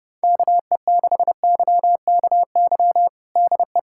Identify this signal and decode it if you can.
Morse code, heard as a keyed tone, about 30 wpm, 700 Hz. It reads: KE6YKY DE
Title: USA Callsign DE Callsign - 6 Characters - 30wpm